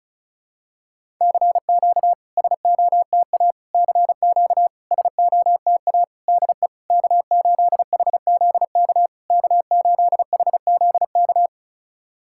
Morse code, heard as a keyed tone, about 36 wpm, 690 Hz. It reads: CQ SOTA CQ SOTA DE K8HZK K8HZK
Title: SOTA QSO Practice 35wpm